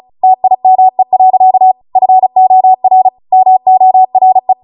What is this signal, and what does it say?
Morse code, heard as a keyed tone, about 36 wpm, 750 Hz. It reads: TIME. FOR MORE